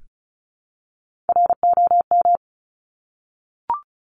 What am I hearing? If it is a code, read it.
Morse code, heard as a keyed tone, about 36 wpm, 700 Hz. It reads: ROM
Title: Top Three Letter Combinations in Words 35wpm